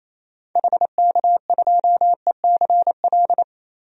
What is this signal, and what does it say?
Morse code, heard as a keyed tone, about 28 wpm, 700 Hz. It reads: HK2ECL
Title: DXCC-100 Call Signs - 6 Characters - 28wpm